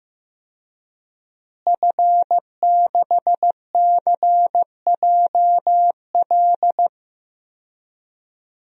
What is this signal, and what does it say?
Morse code, heard as a keyed tone, about 15 wpm, 700 Hz. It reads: F6CJL
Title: DXCC-100 Call Signs - 5 Characters - Repeated 3x - 15wpm